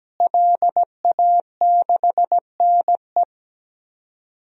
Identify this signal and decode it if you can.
Morse code, heard as a keyed tone, about 17 wpm, 700 Hz. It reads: LA6NE
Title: DXCC-100 Call Signs - 5 Characters - Repeated 3x - 17wpm